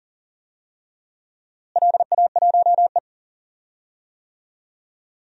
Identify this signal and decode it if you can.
Morse code, heard as a keyed tone, about 40 wpm, 700 Hz. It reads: LA1E
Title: DXCC-100 Call Signs - 4 Characters - Repeated 3x - 40wpm